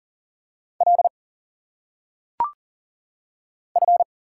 Morse code, heard as a keyed tone, about 40 wpm, 700 Hz. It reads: L F
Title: Single Letters - Phonetic Pronunciation 40wpm